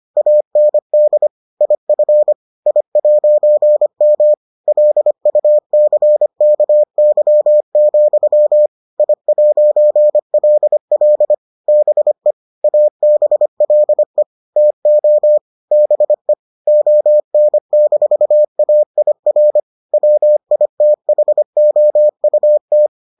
Morse code, heard as a keyed tone, about 25 wpm, 600 Hz. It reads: AND IF I'M LUCKY, I'LL BE ABLE TO BE ON-AIR WITHOUT